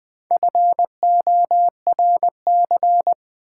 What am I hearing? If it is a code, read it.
Morse code, heard as a keyed tone, about 20 wpm, 700 Hz. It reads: FORC
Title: Build A Word - 1 Letter At A Time - 5-Letter Words - 20wpm